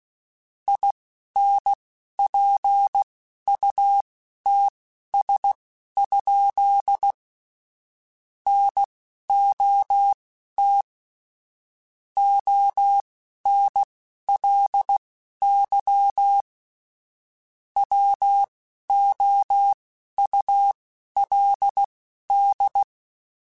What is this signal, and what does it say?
Morse code, heard as a keyed tone, about 16 wpm, spaced as for 11 wpm, 800 Hz. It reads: INPUTS? NOT ONLY WOULD
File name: Technical Subjects / Technical Subjects-mp3s / AND Gate @10